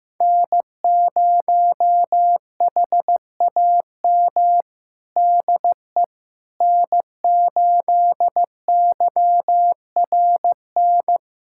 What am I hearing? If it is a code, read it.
Morse code, heard as a keyed tone, about 15 wpm, 700 Hz. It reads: N0HAM DE N8YRN
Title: USA Callsign DE Callsign - 5 Characters - 15wpm